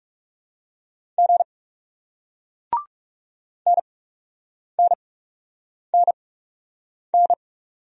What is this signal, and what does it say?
Morse code, heard as a keyed tone, about 40 wpm, spaced as for 8 wpm, 700 Hz. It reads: G NNNN